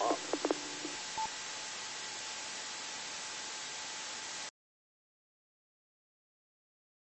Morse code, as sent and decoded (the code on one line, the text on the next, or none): none